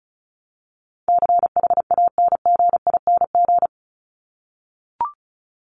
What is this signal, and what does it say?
Morse code, heard as a keyed tone, about 34 wpm, 700 Hz. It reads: CHANGING
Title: ING Suffix - Top 100 Words with Base Words 35wpm